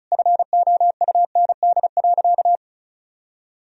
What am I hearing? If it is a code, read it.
Morse code, heard as a keyed tone, about 34 wpm, 700 Hz. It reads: FOUND.
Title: Sentences from Top 1000 Words - Review Part 4 - 35wpm